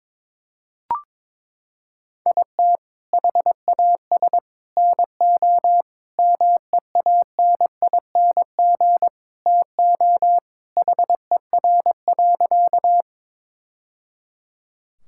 Morse code, heard as a keyed tone, about 22 wpm, 700 Hz. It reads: IT HAS NO MEANING TO HER.
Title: Sentences from Top 1000 Words - Review Part 6 - 22wpm